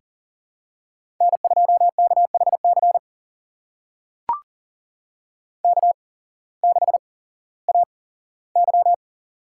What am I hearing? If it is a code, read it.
Morse code, heard as a keyed tone, about 40 wpm, spaced as for 19 wpm, 700 Hz. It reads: N2KHC K6AY
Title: USA Call Signs - 8x Character Spacing - 40wpm